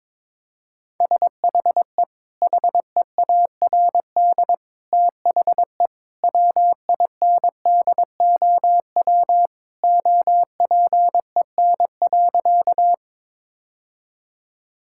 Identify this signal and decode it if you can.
Morse code, heard as a keyed tone, about 21 wpm, 700 Hz. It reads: SHE HEARD THE WINDOW OPEN.
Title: Sentences from Top 1600 Words 22wpm